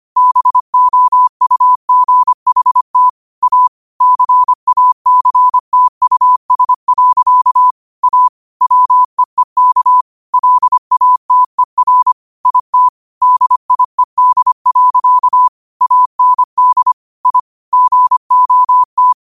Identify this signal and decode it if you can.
Morse code, heard as a keyed tone, about 25 wpm, 1000 Hz. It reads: BOUGHT A CACTUS. A WEEK LATER IT DIED. AND I GOT
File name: qotdfiles / 25wpm